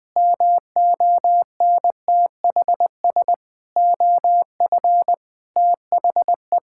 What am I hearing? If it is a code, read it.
Morse code, heard as a keyed tone, about 20 wpm, 700 Hz. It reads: MONTHS OF THE
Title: Sets of 8 Words - 20wpm